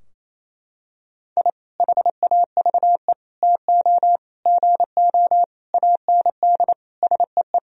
Morse code, heard as a keyed tone, about 28 wpm, 700 Hz. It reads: I HAVE TO GO AND SEE
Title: Sentences from Top 1000 Words - Review Part 4 - 28wpm